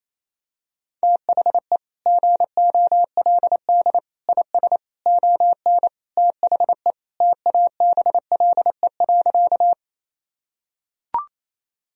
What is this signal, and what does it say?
Morse code, heard as a keyed tone, about 28 wpm, 700 Hz. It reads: THE GOLD IS ON THE TABLE.
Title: Sentences from Top 1000 Words - Review Part 3 - 28wpm